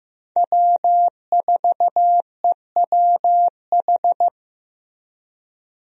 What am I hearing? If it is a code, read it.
Morse code, heard as a keyed tone, about 15 wpm, 700 Hz. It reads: W4EWH
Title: USA Call Signs - 5 Characters - Repeated 3x - 15wpm